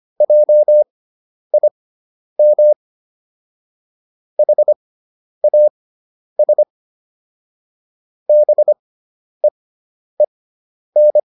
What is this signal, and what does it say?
Morse code, heard as a keyed tone, about 25 wpm, spaced as for 10 wpm, 600 Hz. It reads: JIM HAS BEEN